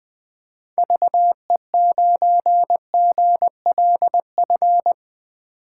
Morse code, heard as a keyed tone, about 21 wpm, 700 Hz. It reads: VE9GLF
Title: K1USN SST QSO Practice - 20wpm